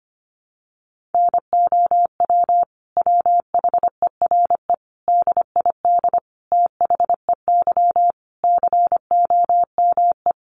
Morse code, heard as a keyed tone, about 25 wpm, 700 Hz. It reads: NOW WHERE DID THEY COME
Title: Sentences from Top 200 Words 25wpm